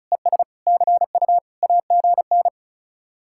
Morse code, heard as a keyed tone, about 44 wpm, 700 Hz. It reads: ES CU AGN
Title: Speed-Racing Common QSO Phrases - 35wpm